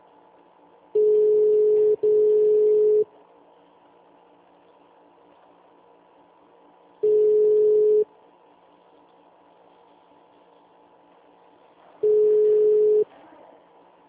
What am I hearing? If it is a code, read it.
Ringback tone (Europe, 425 Hz).
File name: out- -2016- - - .0